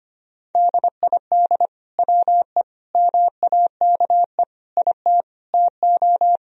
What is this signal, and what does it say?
Morse code, heard as a keyed tone, about 25 wpm, 700 Hz. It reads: DID WE MAKE IT TO
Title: Sentences from Top 1000 Words - Review Part 3 - 25wpm